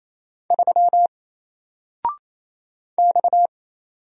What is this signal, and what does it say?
Morse code, heard as a keyed tone, about 27 wpm, 700 Hz. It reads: 3 X
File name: Single-Letter-Number-28wpm